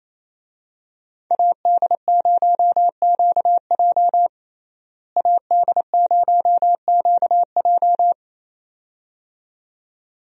Morse code, heard as a keyed tone, about 28 wpm, 700 Hz. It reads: AD0QJ AD0QJ
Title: USA Call Signs - 5 Characters - Repeated 2x - 28wpm